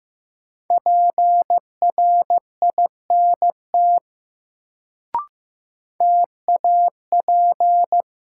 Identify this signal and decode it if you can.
Morse code, heard as a keyed tone, about 15 wpm, 700 Hz. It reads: PRINT TAP
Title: Sentences from Top 1400 Words - New Word Drill Alt - 15wpm